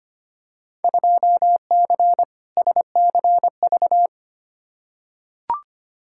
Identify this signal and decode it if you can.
Morse code, heard as a keyed tone, about 25 wpm, 700 Hz. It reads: 2C SCV
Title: Field Day Exchanges Encore - 25wpm